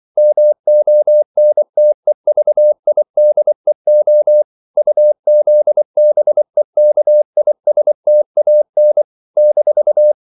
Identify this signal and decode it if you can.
Morse code, heard as a keyed tone, about 24 wpm, 600 Hz. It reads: MONTEVIDEO UZBEKISTAN -
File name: CW / Practice / NationalCapitals24